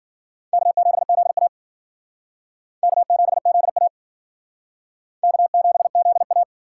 Morse code, heard as a keyed tone, about 60 wpm, 700 Hz. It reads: K7ZA K7ZA K7ZA